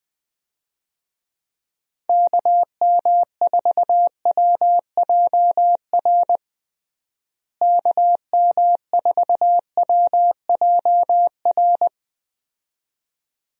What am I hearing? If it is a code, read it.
Morse code, heard as a keyed tone, about 21 wpm, 700 Hz. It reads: KM4WJR KM4WJR
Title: USA Call Signs - 6 Characters - Repeated 2x - 20wpm